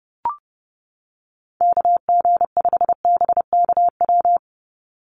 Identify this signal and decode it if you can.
Morse code, heard as a keyed tone, about 29 wpm, 700 Hz. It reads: KG5BKW
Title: USA Call Signs 30wpm